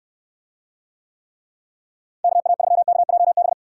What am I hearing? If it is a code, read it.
Morse code, heard as a keyed tone, about 69 wpm, 700 Hz. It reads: KA3KJB